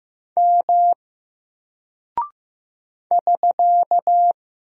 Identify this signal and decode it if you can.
Morse code, heard as a keyed tone, about 15 wpm, 700 Hz. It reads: M <SK>
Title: Single Character (Letter:Number:Punctuation:Prosign) Alt 15wpm